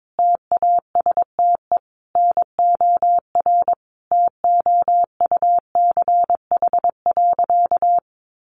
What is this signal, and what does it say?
Morse code, heard as a keyed tone, about 22 wpm, 700 Hz. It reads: TASTE NOR TOUCH.
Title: Sentences from Top 1500 Words 22wpm